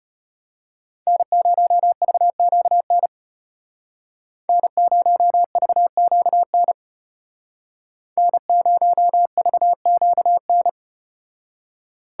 Morse code, heard as a keyed tone, about 33 wpm, 700 Hz. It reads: N0VQN N0VQN N0VQN